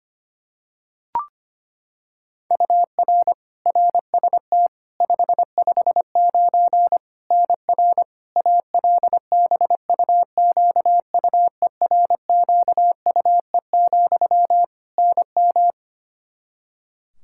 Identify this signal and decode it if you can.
Morse code, heard as a keyed tone, about 25 wpm, 700 Hz. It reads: UR RST 559 NR ALBUQUERQUE, NM
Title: Practice QSO - Exchange 1 - Beginning - 25wpm